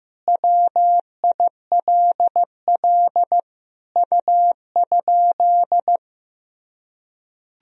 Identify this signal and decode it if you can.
Morse code, heard as a keyed tone, about 15 wpm, 700 Hz. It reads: WILL U?
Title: Sentences from Top 400 Words and Abbreviations 15wpm